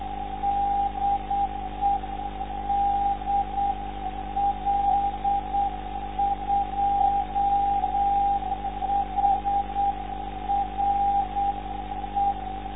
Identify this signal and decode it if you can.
Morse code, heard as a keyed tone, about 8 wpm, 800 Hz. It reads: DE DL2HRE
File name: lf zvoki / s52ab dl2hre